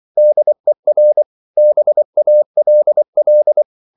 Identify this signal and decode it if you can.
Morse code, heard as a keyed tone, about 24 wpm, 600 Hz. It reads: DER BALL